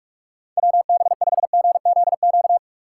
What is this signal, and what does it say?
Morse code, heard as a keyed tone, about 46 wpm, 700 Hz. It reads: WB5GZQ
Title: USA Call Signs - 6 Characters - 4x Character Spacing - 45wpm